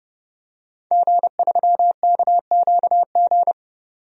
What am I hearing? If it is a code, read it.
Morse code, heard as a keyed tone, about 30 wpm, 700 Hz. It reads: G3KQG
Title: DXCC-100 Call Signs - 5 Characters - 30wpm